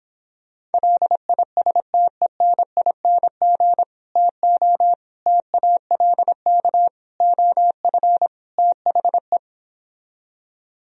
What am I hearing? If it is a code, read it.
Morse code, heard as a keyed tone, about 29 wpm, 700 Hz. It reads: LISTENING TO TALK OF THE